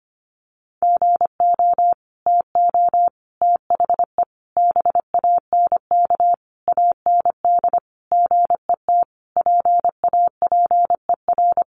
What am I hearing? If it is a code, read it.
Morse code, heard as a keyed tone, about 25 wpm, 700 Hz. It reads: GO TO THE BANK AND GET PAPER
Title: Sentences from Top 1500 Words 25wpm